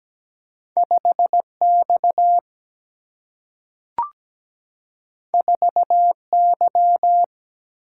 Morse code, heard as a keyed tone, about 17 wpm, 700 Hz. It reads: 5 X 4 Y
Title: Single-Letter Single-Number 17wpm